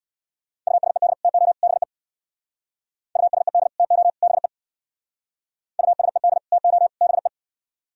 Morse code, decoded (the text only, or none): USED TO BE USED TO BE USED TO BE